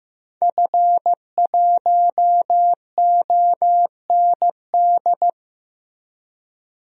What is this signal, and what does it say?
Morse code, heard as a keyed tone, about 15 wpm, 700 Hz. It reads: F1OND
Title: DXCC-100 Call Signs - 5 Characters - Repeated 2x - 15wpm